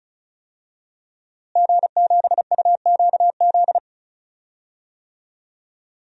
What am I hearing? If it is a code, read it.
Morse code, heard as a keyed tone, about 34 wpm, 700 Hz. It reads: G7UQZ
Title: DXCC-100 Call Signs - 5 Characters - Repeated 2x - 35wpm